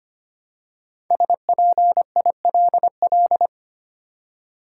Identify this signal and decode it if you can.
Morse code, heard as a keyed tone, about 25 wpm, 700 Hz. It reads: SPILL
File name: 5 Letter Words - First 3 Letters Given - 25wpm